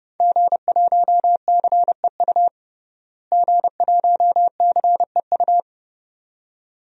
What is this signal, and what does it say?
Morse code, heard as a keyed tone, about 32 wpm, 700 Hz. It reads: G1CEU G1CEU
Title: DXCC-100 Call Signs - 5 Characters - Repeated 2x - 30wpm